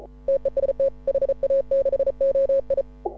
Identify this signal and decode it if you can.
Morse code, heard as a keyed tone, about 34 wpm, 530 Hz. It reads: TEST HA6OI